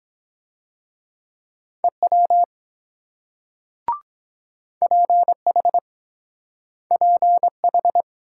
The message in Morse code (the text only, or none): EW PH PH